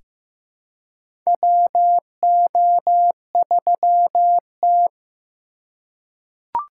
Morse code, heard as a keyed tone, about 15 wpm, 700 Hz. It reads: WO3T
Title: USA Call Signs - 4 Characters - 3x Character Spacing - 15wpm